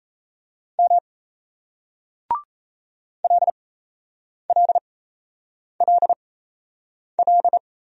Morse code, heard as a keyed tone, about 38 wpm, spaced as for 8 wpm, 700 Hz. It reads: M LLLL